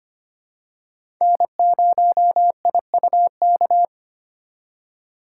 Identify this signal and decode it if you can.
Morse code, heard as a keyed tone, about 25 wpm, 700 Hz. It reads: N0IUK
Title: USA Call Signs - 5 Characters - 2x Character Spacing - 25wpm